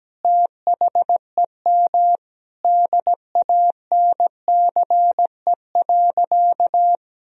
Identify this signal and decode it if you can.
Morse code, heard as a keyed tone, about 17 wpm, 700 Hz. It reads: THEM DANCE.
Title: Sentences from Top 1000 Words - Review Part 4 - 17wpm